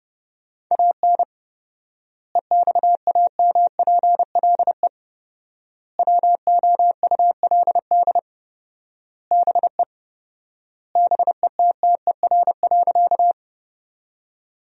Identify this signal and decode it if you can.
Morse code, heard as a keyed tone, about 30 wpm, 700 Hz. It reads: AN EXAMPLE WOULD BE BETTER.
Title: 4x Word-Spacing - Sentences from Top 400 Words - 30wpm